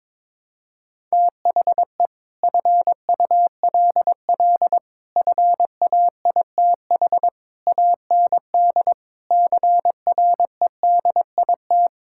Morse code, heard as a keyed tone, about 22 wpm, 700 Hz. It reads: THE FULL FAITH AND CREDIT
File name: practice / 22wpm / Sets of 8 Words - 22wpm